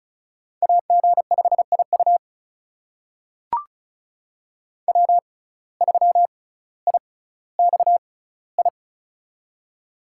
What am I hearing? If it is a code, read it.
Morse code, heard as a keyed tone, about 36 wpm, spaced as for 18 wpm, 700 Hz. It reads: AG5IU W3IXI
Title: USA Call Signs - 5 Characters - 6x Character Spacing - 35wpm